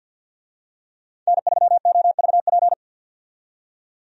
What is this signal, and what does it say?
Morse code, heard as a keyed tone, about 52 wpm, 700 Hz. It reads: N3OVP
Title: USA Call Signs - 5 Characters - 50wpm